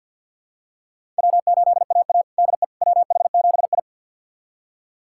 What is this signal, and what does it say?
Morse code, heard as a keyed tone, about 52 wpm, 700 Hz. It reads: W8AA DE WH7I
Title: USA Callsign DE Callsign - 4 Characters - 50wpm